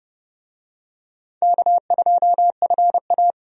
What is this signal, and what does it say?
Morse code, heard as a keyed tone, about 30 wpm, 700 Hz. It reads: K2FA
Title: USA Call Signs - 4 Characters - 30wpm